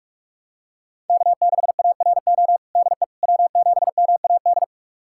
Morse code, heard as a keyed tone, about 47 wpm, 700 Hz. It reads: K6ARO DE W7MAD
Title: USA Callsign DE Callsign - 5 Characters - 45wpm